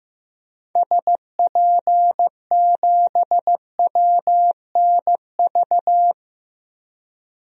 Morse code, heard as a keyed tone, about 15 wpm, 700 Hz. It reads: SP7WNV
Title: DXCC-100 Call Signs - 6 Characters - 15wpm